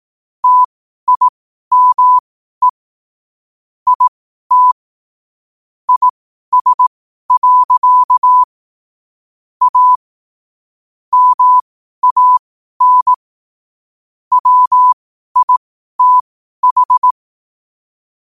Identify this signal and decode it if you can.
Morse code, heard as a keyed tone, about 18 wpm, spaced as for 12 wpm, 1000 Hz. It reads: TIME IT IS. A MAN WITH
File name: qotdfiles / 12wpm